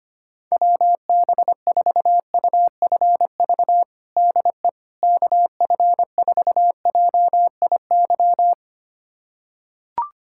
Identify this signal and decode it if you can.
Morse code, heard as a keyed tone, about 25 wpm, 700 Hz. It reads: WB4UFV DE KF4JIY
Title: USA Callsign DE Callsign - 6 Characters - 25wpm